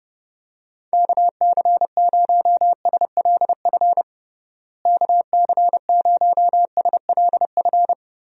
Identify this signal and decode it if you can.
Morse code, heard as a keyed tone, about 30 wpm, 700 Hz. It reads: KC0SLF KC0SLF
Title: USA Call Signs - 6 Characters - Repeated 3x - 30wpm